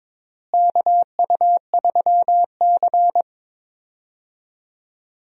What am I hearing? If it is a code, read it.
Morse code, heard as a keyed tone, about 21 wpm, 700 Hz. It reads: KU3C
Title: USA Call Signs - 4 Characters - Repeated 3x - 22wpm